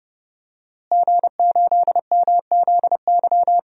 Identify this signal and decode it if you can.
Morse code, heard as a keyed tone, about 30 wpm, 700 Hz. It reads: G8MZY
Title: DXCC-100 Call Signs - 5 Characters - Repeated 2x - 30wpm